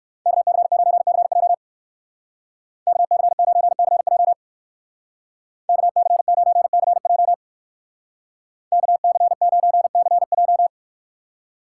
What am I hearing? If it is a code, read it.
Morse code, heard as a keyed tone, about 56 wpm, 700 Hz. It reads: KC9CJ KC9CJ KC9CJ KC9CJ